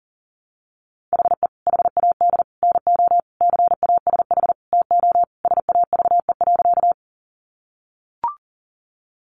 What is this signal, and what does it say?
Morse code, heard as a keyed tone, about 40 wpm, 700 Hz. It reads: HE HAD NO CASH TO SAVE.
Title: Sentences from Top 1100 Words 40wpm